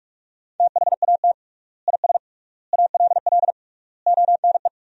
Morse code, heard as a keyed tone, about 46 wpm, 700 Hz. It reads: THAT IS ALL ONE
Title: 3x Word-Spacing - Sentences from Top 200 Words - 45wpm